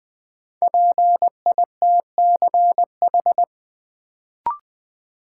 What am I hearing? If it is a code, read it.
Morse code, heard as a keyed tone, about 21 wpm, 700 Hz. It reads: PITCH
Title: Sentences from Top 1300 Words - New Word Drill - 20wpm